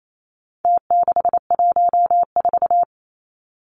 Morse code, heard as a keyed tone, about 28 wpm, 700 Hz. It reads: T614
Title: Numbers - Contest Style - 4-Digit 28wpm